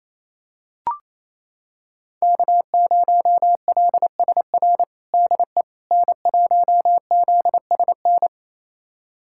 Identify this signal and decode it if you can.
Morse code, heard as a keyed tone, about 28 wpm, 700 Hz. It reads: K0LSR DE N1ZSN
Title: USA Callsign DE Callsign - 5 Characters - 28wpm